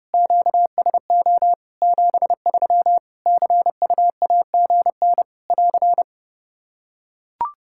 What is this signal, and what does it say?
Morse code, heard as a keyed tone, about 30 wpm, 700 Hz. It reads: QSO 73 CUAGN <AR>
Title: K1USN SST QSO Practice 30wpm